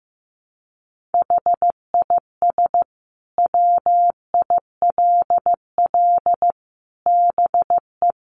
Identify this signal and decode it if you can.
Morse code, heard as a keyed tone, about 15 wpm, 700 Hz. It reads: HIS WILL BE
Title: Sentences from Top 500 Words 15wpm